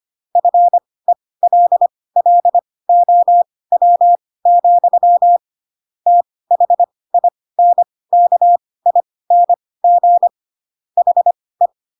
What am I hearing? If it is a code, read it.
Morse code, heard as a keyed tone, about 25 wpm, spaced as for 18 wpm, 700 Hz. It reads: FELLOW, THINKING HE
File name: Pinocchio / Pinocchio ch6 12 25 ik1hge